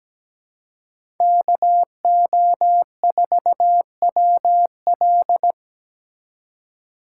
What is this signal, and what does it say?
Morse code, heard as a keyed tone, about 17 wpm, 700 Hz. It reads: KO4WL
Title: USA Call Signs - 5 Characters - 3x Character Spacing - 17wpm